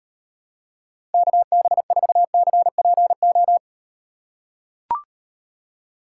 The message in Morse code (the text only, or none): KB4CPO